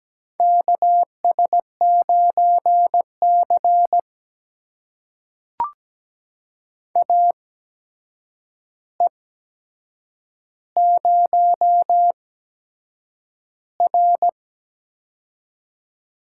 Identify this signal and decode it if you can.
Morse code, heard as a keyed tone, about 17 wpm, spaced as for 7 wpm, 700 Hz. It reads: KS9C AE0R
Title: USA Call Signs - 4 Characters - 8x Character Spacing - 17wpm